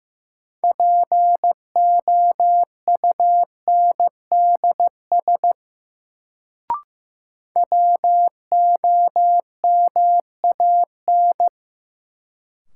Morse code, heard as a keyed tone, about 15 wpm, 700 Hz. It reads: POUNDS WOMAN
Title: Sentences from Top 1000 Words - Review Part 2 - New Word Drill - 15wpm